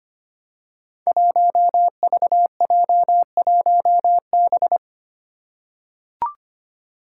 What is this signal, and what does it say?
Morse code, heard as a keyed tone, about 25 wpm, 700 Hz. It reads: 1VJ1B
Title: Five Challenging Letter-Number Combinations 25wpm